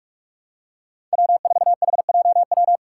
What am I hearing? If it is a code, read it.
Morse code, heard as a keyed tone, about 42 wpm, 700 Hz. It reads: W4HJW
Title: Five Challenging Letter-Number Combinations 45wpm